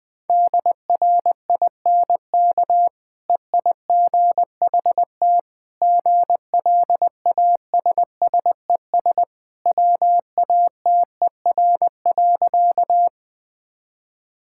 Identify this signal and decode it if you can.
Morse code, heard as a keyed tone, about 21 wpm, 700 Hz. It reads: DRINK EIGHT GLASSES WATER.
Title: Easy Sentences with Hints - 20wpm